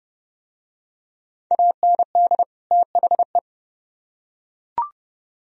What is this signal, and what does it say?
Morse code, heard as a keyed tone, about 29 wpm, 700 Hz. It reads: AND THE